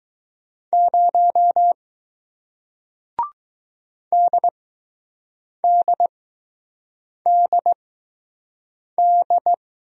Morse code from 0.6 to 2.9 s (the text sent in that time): 0